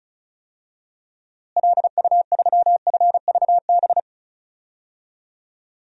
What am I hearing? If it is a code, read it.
Morse code, heard as a keyed tone, about 36 wpm, 700 Hz. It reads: LU3FVB
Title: DXCC-100 Call Signs - 6 Characters - Repeated 3x - 35wpm